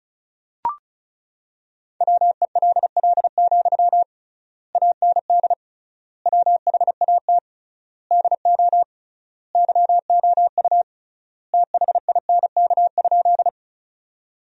Morse code, heard as a keyed tone, about 34 wpm, 700 Hz. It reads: WELL, AND WHAT DO YOU THINK?
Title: 3x Word-Spacing - Sentences from Top 200 Words - 35wpm